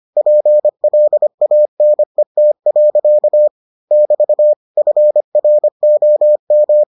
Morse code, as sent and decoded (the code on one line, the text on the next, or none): .--. .-.. .- -. - .-.-.- / -...- / ..-. .-. --- --
PLANET. <BT> FROM